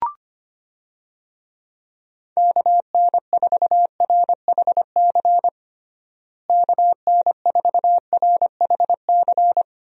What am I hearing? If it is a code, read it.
Morse code, heard as a keyed tone, about 25 wpm, 700 Hz. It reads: KN4RHC KN4RHC
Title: USA Call Signs - 6 Characters - Repeated 3x - 25wpm